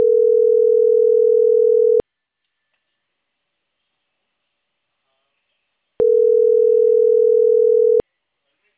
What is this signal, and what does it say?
Ringback tone (North America), 440 + 480 Hz.